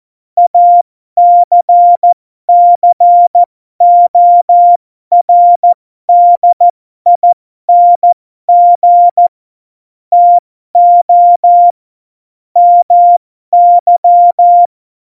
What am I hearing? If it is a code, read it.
Morse code, heard as a keyed tone, about 14 wpm, spaced as for 12 wpm, 700 Hz. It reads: ACCORDING TO MY
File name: Morsekurs / Witze